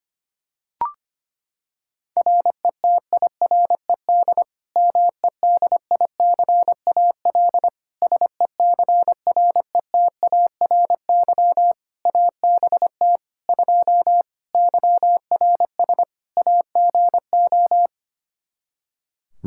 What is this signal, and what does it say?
Morse code, heard as a keyed tone, about 25 wpm, 700 Hz. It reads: RETIRED MEDICAL SECRETARY ABT 2 YRS AGO
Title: Practice QSO - Exchange 3 - More Info - 25wpm